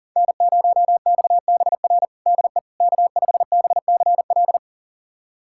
Morse code, heard as a keyed tone, about 40 wpm, 700 Hz. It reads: N0XBR DE K5BCL
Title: USA Callsign DE Callsign - 5 Characters - 40wpm